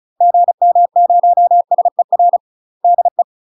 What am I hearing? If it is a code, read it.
Morse code, heard as a keyed tone, about 34 wpm, 700 Hz. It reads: GM0SER DE
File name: UK MASTER QSOs TRAINING / 35WPM / UK MASTER QSOs 0003